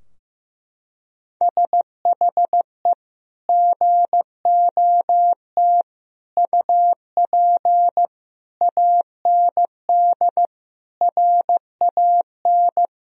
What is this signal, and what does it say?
Morse code, heard as a keyed tone, about 15 wpm, 700 Hz. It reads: SHE GOT UP AND RAN
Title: Sentences from Top 1000 Words - Review Part 3 - 15wpm